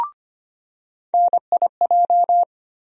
Morse code, heard as a keyed tone, about 25 wpm, 700 Hz. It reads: NIJ
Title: USA Call Sign Postfixes - 3 Letters 25wpm